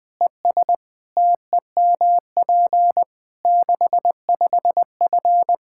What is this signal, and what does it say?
Morse code, heard as a keyed tone, about 20 wpm, 700 Hz. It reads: ES TEMP 65F
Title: Practice QSO - Exchange 2 - Info - 20wpm